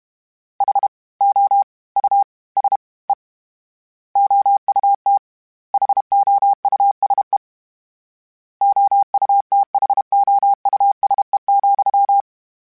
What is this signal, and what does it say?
Morse code, heard as a keyed tone, about 32 wpm, 800 Hz. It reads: H O U S E OUT HOUSE OUTHOUSE,
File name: General / General-mp3s / Houses @30